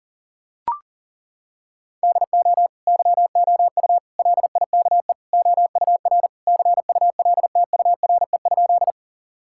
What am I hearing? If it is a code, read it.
Morse code, heard as a keyed tone, about 40 wpm, 690 Hz. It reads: DO YOU LIKE OUR CULTURE?
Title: Sentences from Top 1000 Words - Review Part 6 - 40wpm